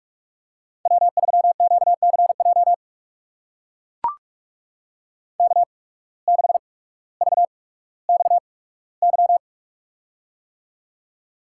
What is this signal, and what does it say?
Morse code, heard as a keyed tone, about 44 wpm, spaced as for 19 wpm, 700 Hz. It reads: W3QCJ K6VXY
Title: USA Call Signs - 8x Character Spacing - 45wpm